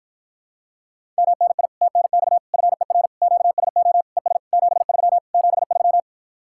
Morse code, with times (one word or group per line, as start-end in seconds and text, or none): 1.2-1.7 MNI
1.8-2.4 TNX
2.5-3.1 FER
3.2-4.0 QSO
4.2-4.4 ES
4.5-5.2 73
5.3-6.0 73